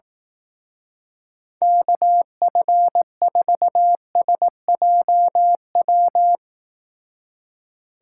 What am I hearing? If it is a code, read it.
Morse code, heard as a keyed tone, about 18 wpm, 700 Hz. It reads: KF4SJW
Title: Speed Racing - USA Call Signs 15wpm